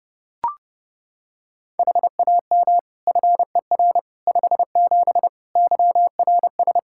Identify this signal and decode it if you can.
Morse code, heard as a keyed tone, about 30 wpm, 700 Hz. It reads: HAM FER 57 YRS
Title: K1USN SST QSO Practice 30wpm